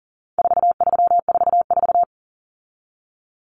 Morse code, heard as a keyed tone, about 40 wpm, 700 Hz. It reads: 4344
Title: Common Cut Numbers - 4-Digit 40wpm